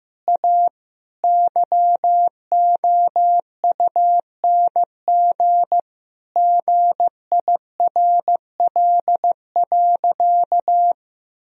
Morse code, heard as a keyed tone, about 15 wpm, 700 Hz. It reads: A YOUNG GIRL.
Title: Sentences from Top 1000 Words - Review Part 4 - 15wpm